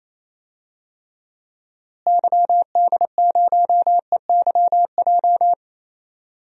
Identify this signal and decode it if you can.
Morse code, heard as a keyed tone, about 28 wpm, 700 Hz. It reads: YD0EYJ
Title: DXCC-100 Call Signs - 6 Characters - Repeated 2x - 28wpm